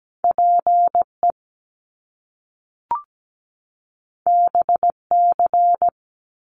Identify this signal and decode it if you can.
Morse code, heard as a keyed tone, about 17 wpm, 700 Hz. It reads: PE BC
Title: Canadian Provinces and Territories 17wpm